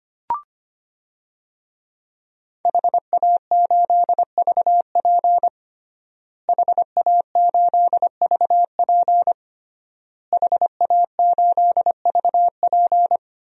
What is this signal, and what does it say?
Morse code, heard as a keyed tone, about 25 wpm, 700 Hz. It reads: HA8VP HA8VP HA8VP
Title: DXCC-100 Call Signs - 5 Characters - Repeated 3x - 25wpm